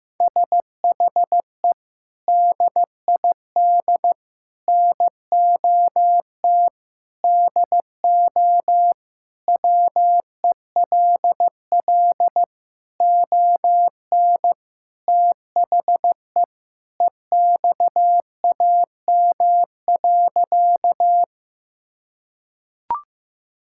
Morse code, heard as a keyed tone, about 16 wpm, 700 Hz. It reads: SHE DID NOT DO WELL ON THE EXAM.
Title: Sentences from Top 1000 Words - Review Part 5 - 15wpm